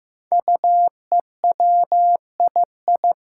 Morse code, heard as a keyed tone, about 15 wpm, 700 Hz. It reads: UEWII
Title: Five Challenging Letter-Number Combinations 15wpm